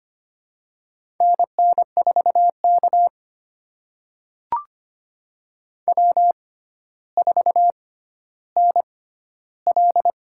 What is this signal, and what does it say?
Morse code, heard as a keyed tone, about 25 wpm, spaced as for 13 wpm, 700 Hz. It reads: NN4K W4NL
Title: USA Call Signs - 4 Characters - 6x Character Spacing - 25wpm